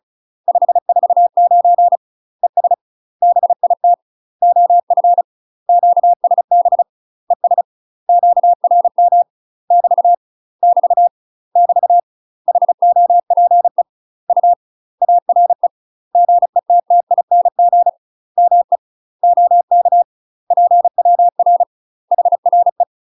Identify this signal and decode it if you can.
Morse code, heard as a keyed tone, about 36 wpm, 700 Hz. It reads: 549 ES BIT OF QSB ES QRM <BT> <BT> <BT> HOPE U ARE GETTING ME OK PWR HRE
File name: UK MASTER QSOs TRAINING / 35WPM / UK MASTER QSOs 0000